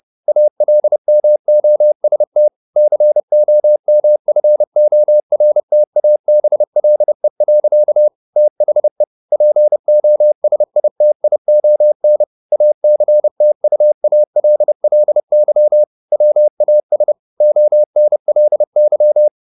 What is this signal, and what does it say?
Morse code, heard as a keyed tone, about 30 wpm, 600 Hz. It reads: ALMOST COMFORTABLE. THE POSITION ACTUALLY WAS ONLY
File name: the old man and the sea / the old man and the sea 1 30wpm